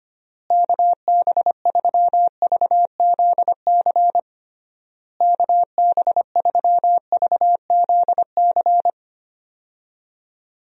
Morse code, heard as a keyed tone, about 25 wpm, 700 Hz. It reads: KB3VZC KB3VZC
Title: USA Call Signs - 6 Characters - Repeated 3x - 25wpm